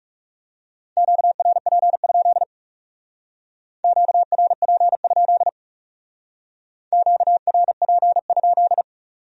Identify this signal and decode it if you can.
Morse code, heard as a keyed tone, about 40 wpm, 700 Hz. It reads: QRP? QRP? QRP?